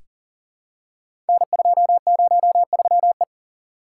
Morse code, heard as a keyed tone, about 39 wpm, 700 Hz. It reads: N203E
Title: Code Groups - 5 Letter:Numbers - 40wpm